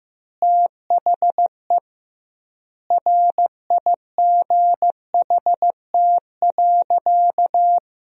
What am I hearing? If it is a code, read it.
Morse code, heard as a keyed tone, about 15 wpm, 700 Hz. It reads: THE RIGHT.
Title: 2x Word-Spacing - Sentences from Top 300 Words - 15wpm